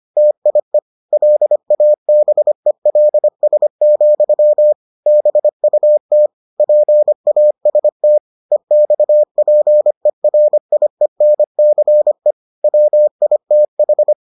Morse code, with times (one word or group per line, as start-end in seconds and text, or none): 0.2-0.8 TIE
1.1-4.7 LABELS,
5.1-6.3 BUT
6.6-8.2 PAST
8.5-12.3 EXPERIENCE
12.6-14.1 WITH